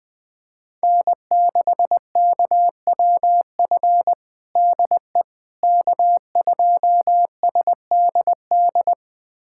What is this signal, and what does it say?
Morse code, heard as a keyed tone, about 20 wpm, 700 Hz. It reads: N6KWF DE K2SDD
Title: USA Callsign DE Callsign - 5 Characters - 20wpm